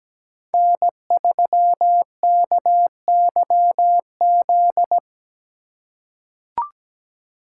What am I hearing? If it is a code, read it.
Morse code, heard as a keyed tone, about 17 wpm, 700 Hz. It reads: N3KYZ
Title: USA Call Signs - 5 Characters - 3x Character Spacing - 17wpm